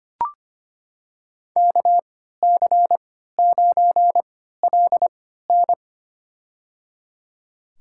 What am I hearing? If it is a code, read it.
Morse code, heard as a keyed tone, about 25 wpm, spaced as for 14 wpm, 700 Hz. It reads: KC9LN
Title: USA Call Signs - 3x Character Spacing - 25wpm